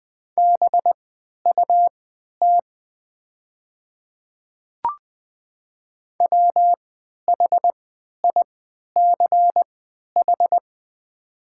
Morse code, heard as a keyed tone, about 20 wpm, spaced as for 11 wpm, 700 Hz. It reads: BUT WHICH
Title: 3x Character Spacing - Top 100 Words - 20wpm